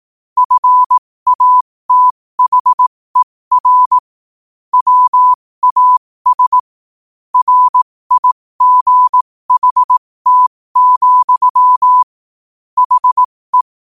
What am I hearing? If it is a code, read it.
Morse code, heard as a keyed tone, about 18 wpm, spaced as for 15 wpm, 1000 Hz. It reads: FATHER WAS RIGHT, HE
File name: 15wpm